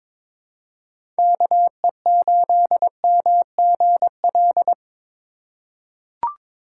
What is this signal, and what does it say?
Morse code, heard as a keyed tone, about 22 wpm, 700 Hz. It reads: KE8MGL
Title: USA Call Signs - 6 Characters - 4x Character Spacing - 22wpm